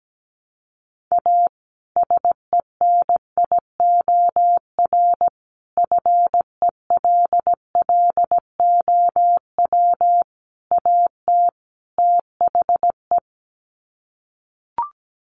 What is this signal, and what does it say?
Morse code, heard as a keyed tone, about 17 wpm, 700 Hz. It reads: A SENIOR FELLOW AT THE
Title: Sets of 5 Words Encore 17wpm